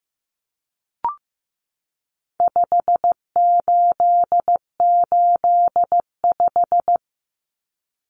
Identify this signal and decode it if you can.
Morse code, heard as a keyed tone, about 15 wpm, 700 Hz. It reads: 5885
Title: Numbers - 4-Digit 15wpm